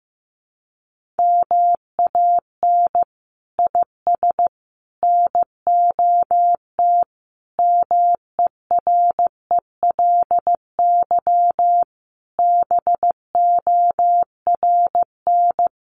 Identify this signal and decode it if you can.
Morse code, heard as a keyed tone, about 15 wpm, 700 Hz. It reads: MAN IS NOT MERELY BORN
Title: Sentences from Top 1400 Words 15wpm